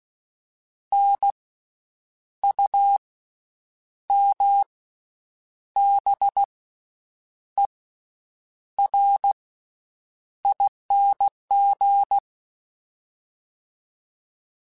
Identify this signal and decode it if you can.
Morse code, heard as a keyed tone, about 16 wpm, spaced as for 7 wpm, 800 Hz. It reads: NUMBERING
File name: Special INGs / Special INGs-mp3s / VerySlowToFaster Ing @15